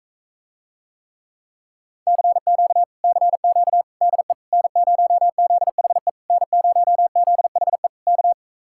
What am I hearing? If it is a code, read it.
Morse code, heard as a keyed tone, about 45 wpm, 700 Hz. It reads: CQ CQ DE N0ZHE N0ZHE K